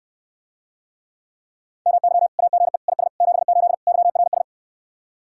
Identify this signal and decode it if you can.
Morse code, heard as a keyed tone, about 67 wpm, 700 Hz. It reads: MY AGE IS 78 YRS